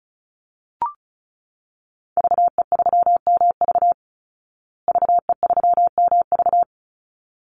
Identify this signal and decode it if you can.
Morse code, heard as a keyed tone, about 34 wpm, 700 Hz. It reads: VE3MV VE3MV
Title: Call Signs Repeated 3x - 35wpm